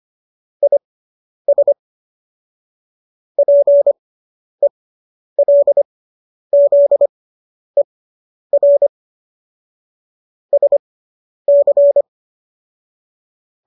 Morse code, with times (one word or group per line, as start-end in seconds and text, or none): 0.6-1.7 IS
3.4-8.9 PELZER
10.5-12.0 SC